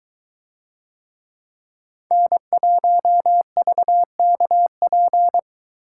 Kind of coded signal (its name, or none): Morse code